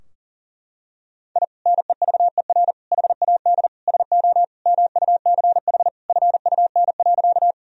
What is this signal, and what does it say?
Morse code, heard as a keyed tone, about 37 wpm, 700 Hz. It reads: I NEVER HAD SO MUCH FUN.
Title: Sentences from Top 1000 Words - Review Part 4 - 40wpm